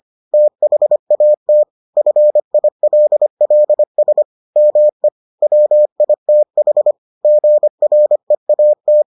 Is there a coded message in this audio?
Yes: Morse code from 0.3 to 9.0 s, heard as a keyed tone, about 25 wpm, 600 Hz. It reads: THAT FILLS ME WITH GREAT